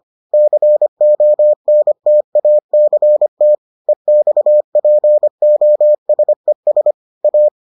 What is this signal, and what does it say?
Morse code, heard as a keyed tone, about 25 wpm, 600 Hz. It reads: CONTACT EXPOSES A